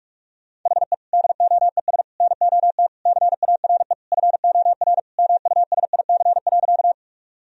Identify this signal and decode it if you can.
Morse code, heard as a keyed tone, about 46 wpm, 700 Hz. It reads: HE DOES NOT CARE FOR MUSIC.
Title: Sentences from Top 1000 Words - Review Part 1 - 45wpm